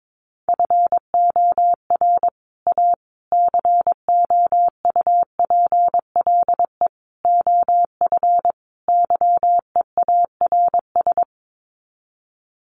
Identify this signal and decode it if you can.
Morse code, heard as a keyed tone, about 22 wpm, 700 Hz. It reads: FOR A COUPLE OF YEARS
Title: Sets of 5 Words Encore 22wpm